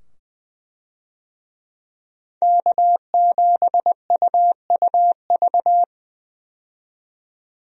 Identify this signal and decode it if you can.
Morse code, heard as a keyed tone, about 20 wpm, 700 Hz. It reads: K7UUV
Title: USA Call Signs - 5 Characters - 20wpm